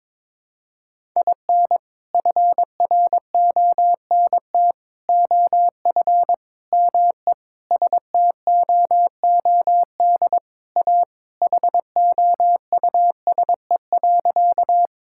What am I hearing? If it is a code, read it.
Morse code, heard as a keyed tone, about 21 wpm, 700 Hz. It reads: IN FRONT OF ME STOOD A HOUSE.
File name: practice / 22wpm / Sentences with 4-Word Phrases - 22wpm